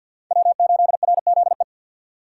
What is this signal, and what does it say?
Morse code, heard as a keyed tone, about 49 wpm, 700 Hz. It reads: W8RZE
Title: USA Call Signs - 5 Characters - 3x Character Spacing - 50wpm